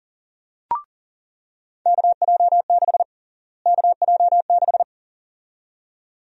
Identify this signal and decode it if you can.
Morse code, heard as a keyed tone, about 40 wpm, 700 Hz. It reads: KJ6 KJ6
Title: USA Call Sign Prefixes - Repeated 2x - 40wpm